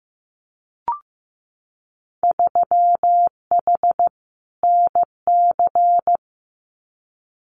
Morse code, heard as a keyed tone, about 15 wpm, 700 Hz. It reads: 3H NC
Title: Winter Field Day Exchanges 15wpm